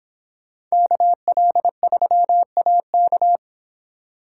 Morse code, heard as a keyed tone, about 25 wpm, 700 Hz. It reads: KL3AK